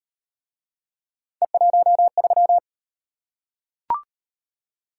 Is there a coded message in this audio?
Yes: Morse code from 1.4 to 2.6 s, heard as a keyed tone, about 35 wpm, 700 Hz. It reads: E13